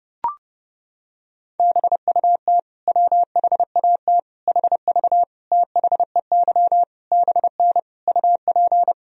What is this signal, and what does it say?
Morse code, heard as a keyed tone, about 30 wpm, 700 Hz. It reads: BUT WHAT HV THEY BN UP
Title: Sentences from Top 100 Words and Abbreviations 30wpm